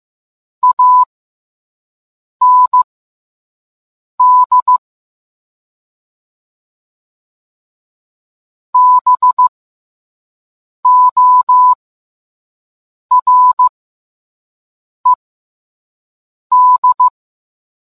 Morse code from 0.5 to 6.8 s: AND